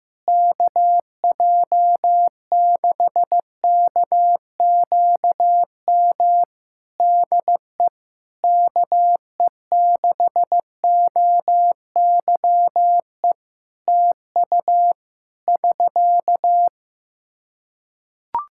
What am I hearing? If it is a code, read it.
Morse code, heard as a keyed tone, about 15 wpm, 700 Hz. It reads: KJ6KQM DE KE6OYE TU <SK>
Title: Practice QSO - Exchange 4 - Ending - 15wpm